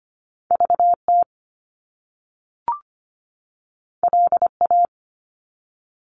Morse code, heard as a keyed tone, about 25 wpm, 700 Hz. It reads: VT LA
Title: US States 25wpm